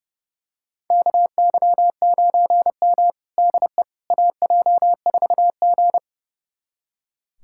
Morse code, heard as a keyed tone, about 32 wpm, 700 Hz. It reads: KY9M DE AJ4G
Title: USA Callsign DE Callsign - 4 Characters - 30wpm